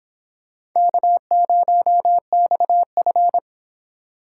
Morse code, heard as a keyed tone, about 24 wpm, 700 Hz. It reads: K0XF